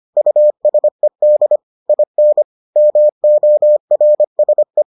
Morse code, heard as a keyed tone, about 25 wpm, 600 Hz. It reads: USED IN MORSE